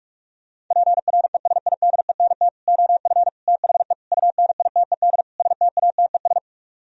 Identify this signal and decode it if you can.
Morse code, heard as a keyed tone, about 40 wpm, 700 Hz. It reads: PRESIDENT OF THE UNITED STATES